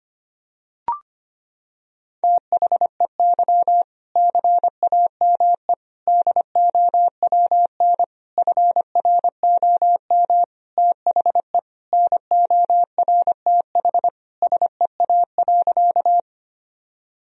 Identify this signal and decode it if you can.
Morse code, heard as a keyed tone, about 25 wpm, 700 Hz. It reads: THEY CAME DOWN FROM THE NORTH SEA.
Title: Sentences from Top 1000 Words - Review Part 1 - 25wpm